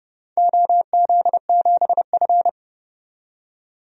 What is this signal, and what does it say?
Morse code, heard as a keyed tone, about 29 wpm, 700 Hz. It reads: OZ7F
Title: DXCC-100 Call Signs - 4 Characters - 30wpm